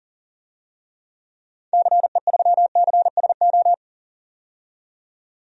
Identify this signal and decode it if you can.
Morse code, heard as a keyed tone, about 40 wpm, 700 Hz. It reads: CE3CSO
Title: DXCC-100 Call Signs - 6 Characters - 40wpm